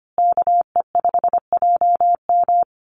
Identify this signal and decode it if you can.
Morse code, heard as a keyed tone, about 25 wpm, 700 Hz. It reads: KE5JM
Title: USA Call Signs Repeated 2x 25wpm